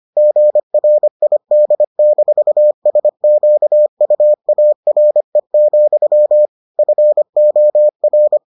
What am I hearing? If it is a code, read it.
Morse code, heard as a keyed tone, about 25 wpm, 600 Hz. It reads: GRID-SQUARE, FOR